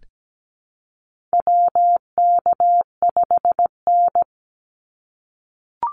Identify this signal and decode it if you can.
Morse code, heard as a keyed tone, about 17 wpm, 700 Hz. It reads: WK5N
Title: Call Signs - 17wpm